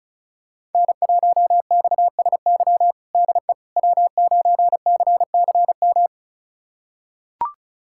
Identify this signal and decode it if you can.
Morse code, heard as a keyed tone, about 34 wpm, 700 Hz. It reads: N1XSY DE W9CCM
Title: USA Callsign DE Callsign - 5 Characters - 35wpm